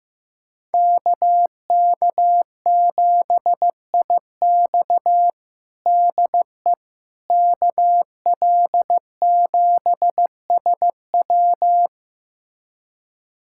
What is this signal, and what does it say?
Morse code, heard as a keyed tone, about 15 wpm, 700 Hz. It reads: KK7IX DE KL7SW
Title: USA Callsign DE Callsign - 5 Characters - 15wpm